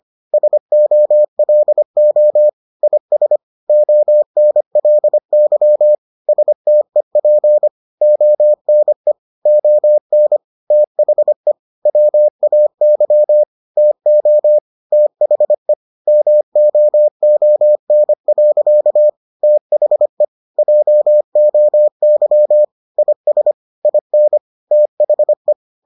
Morse code, heard as a keyed tone, about 25 wpm, 600 Hz. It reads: SOLO IS ONLY STEP ONE ON THE WAY TO THE MOON. THE JOY IS IN THE